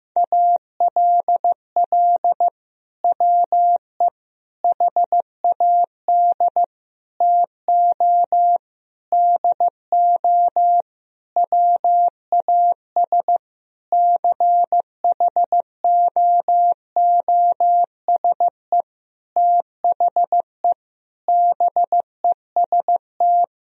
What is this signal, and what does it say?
Morse code, heard as a keyed tone, about 15 wpm, 700 Hz. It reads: ALL WE HAD TO DO WAS CHOOSE THE BEST
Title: Sentences from Top 1000 Words - Review Part 4 - 15wpm